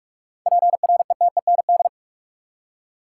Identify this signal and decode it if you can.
Morse code, heard as a keyed tone, about 47 wpm, 700 Hz. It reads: PRETEND
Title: Seven Letter Words - 45wpm